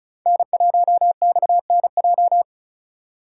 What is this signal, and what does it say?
Morse code, heard as a keyed tone, about 36 wpm, 700 Hz. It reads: N1XNJ
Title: USA Call Signs - 5 Characters - 4x Character Spacing - 35wpm